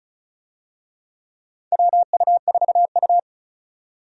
Morse code, heard as a keyed tone, about 36 wpm, 700 Hz. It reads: WU4U
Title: USA Call Signs - 4 Characters - 35wpm